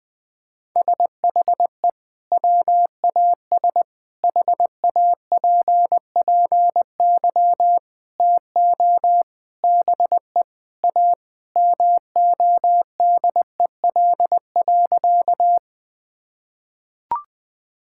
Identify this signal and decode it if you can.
Morse code, heard as a keyed tone, about 20 wpm, 700 Hz. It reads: SHE WAS HAPPY TO BE A MODEL.
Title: Sentences from Top 1000 Words - Review Part 5 - 20wpm